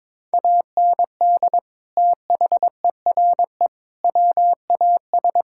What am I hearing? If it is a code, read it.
Morse code, heard as a keyed tone, about 21 wpm, 700 Hz. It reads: AND THERE WAS
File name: practice / 22wpm / Sentences with 3-Word Phrases - 22wpm